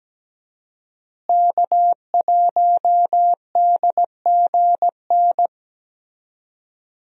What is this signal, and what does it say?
Morse code, heard as a keyed tone, about 17 wpm, 700 Hz. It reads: K1DGN
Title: USA Call Signs - 5 Characters - Repeated 2x - 17wpm